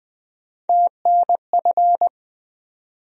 Morse code, heard as a keyed tone, about 20 wpm, 700 Hz. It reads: TNF
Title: USA Call Sign Postfixes - 3 Letters 20wpm